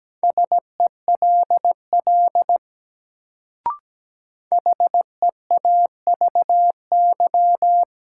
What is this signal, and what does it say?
Morse code, heard as a keyed tone, about 17 wpm, 700 Hz. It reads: SELL HEAVY
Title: Sentences from Top 1100 Words - New Word Drill Alt - 17wpm